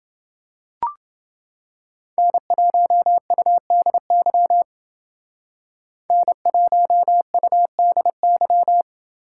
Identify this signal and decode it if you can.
Morse code, heard as a keyed tone, about 28 wpm, 700 Hz. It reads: N1UDY N1UDY